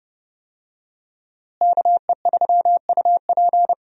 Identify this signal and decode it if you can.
Morse code, heard as a keyed tone, about 29 wpm, 700 Hz. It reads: KE3UP
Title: USA Call Signs - 5 Characters - Repeated 3x - 30wpm